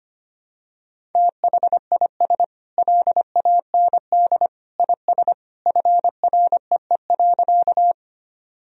Morse code, heard as a keyed tone, about 25 wpm, 700 Hz. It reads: THIS LAND IS FREE.
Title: Sentences from Top 1000 Words - Review Part 1 - 25wpm